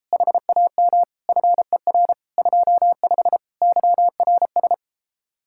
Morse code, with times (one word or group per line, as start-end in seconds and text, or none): 0.1-1.0 HAM
1.3-2.1 FER
2.4-3.4 25
3.6-4.7 YRS